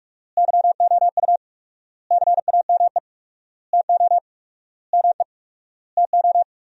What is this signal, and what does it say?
Morse code, heard as a keyed tone, about 46 wpm, 700 Hz. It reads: YOU CAME TO ME TO
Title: 4x Word-Spacing - Sentences from Top 200 Words - 45wpm